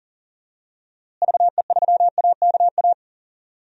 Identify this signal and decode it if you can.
Morse code, heard as a keyed tone, about 40 wpm, 700 Hz. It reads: VE3AKA
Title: DXCC-100 Call Signs - 6 Characters - Repeated 3x - 40wpm